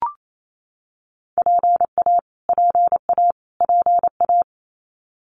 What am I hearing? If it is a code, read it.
Morse code, heard as a keyed tone, about 28 wpm, 700 Hz. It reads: PA PA PA
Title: US States Repeated 3x 28wpm